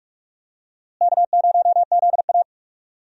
Morse code, heard as a keyed tone, about 46 wpm, 700 Hz. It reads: K0ZA
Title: USA Call Signs - 4 Characters - Repeated 2x - 45wpm